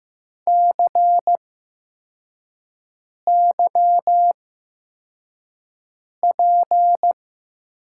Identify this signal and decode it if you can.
Morse code, heard as a keyed tone, about 15 wpm, spaced as for 4 wpm, 700 Hz. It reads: CYP
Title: USA Call Signs - 6 Characters - 8x Character Spacing - 15wpm